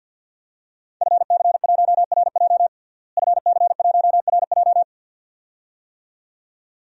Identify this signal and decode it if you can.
Morse code, heard as a keyed tone, about 52 wpm, 700 Hz. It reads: FC1RJ FC1RJ
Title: DXCC-100 Call Signs - 5 Characters - Repeated 2x - 50wpm